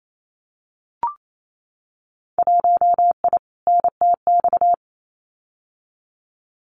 Morse code, heard as a keyed tone, about 28 wpm, 700 Hz. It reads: 1I NTX
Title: Winter Field Day Exchanges Encore 28wpm